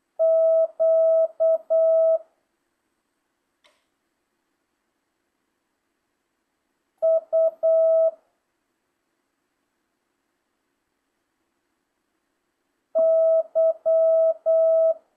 Morse code, heard as a keyed tone, about 8 wpm, 650 Hz. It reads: Q U Y